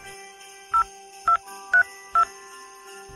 DTMF, heard as telephone touch tones, tones of about 100 ms. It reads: 0232